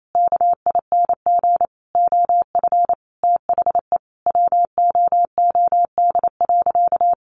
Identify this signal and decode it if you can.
Morse code, heard as a keyed tone, about 28 wpm, 700 Hz. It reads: KING OF THE WOOD.
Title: Sentences from Top 1200 Words 28wpm